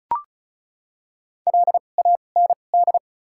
Morse code, heard as a keyed tone, about 36 wpm, spaced as for 25 wpm, 700 Hz. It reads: LAND
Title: 2x Character Spacing - Top 200 Words - 35wpm